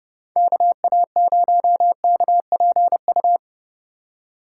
Morse code, heard as a keyed tone, about 36 wpm, 700 Hz. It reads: KA0KPU
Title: Speed Racing - USA 2x3 Call Signs 30wpm